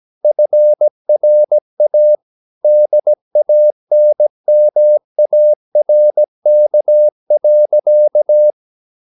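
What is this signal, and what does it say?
Morse code, heard as a keyed tone, about 17 wpm, 600 Hz. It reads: FRA DANMARK.
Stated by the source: Slow (QRS) Morse code at 16 wpm, from CW med Gnister.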